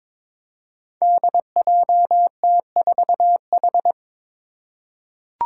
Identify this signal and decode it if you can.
Morse code, heard as a keyed tone, about 22 wpm, 700 Hz. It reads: DJT4H
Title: Five Challenging Letter-Number Combinations 22wpm